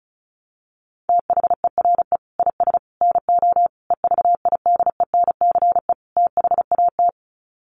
Morse code, heard as a keyed tone, about 34 wpm, 700 Hz. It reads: THERE IS NO EVIDENCE THAT
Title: Sets of 5 Words Encore 35wpm